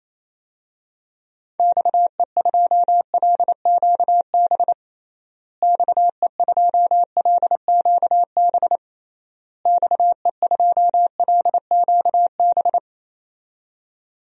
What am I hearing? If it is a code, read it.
Morse code, heard as a keyed tone, about 28 wpm, 700 Hz. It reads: XE2LQB XE2LQB XE2LQB
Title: DXCC-100 Call Signs - 6 Characters - Repeated 3x - 28wpm